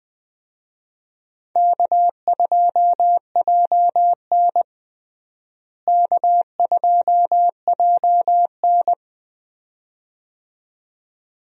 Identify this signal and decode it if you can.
Morse code, heard as a keyed tone, about 20 wpm, 700 Hz. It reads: K2JN K2JN
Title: USA Call Signs - 4 Characters - Repeated 2x - 20wpm